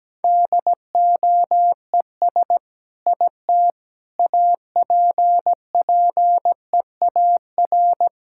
Morse code, heard as a keyed tone, about 17 wpm, 700 Hz. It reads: DOES IT APPEAR
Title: Sentences from Top 1000 Words - Review Part 6 - 17wpm